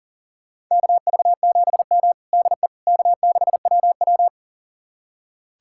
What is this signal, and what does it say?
Morse code, heard as a keyed tone, about 38 wpm, 700 Hz. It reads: KV7M DE K6WW
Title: USA Callsign DE Callsign - 4 Characters - 40wpm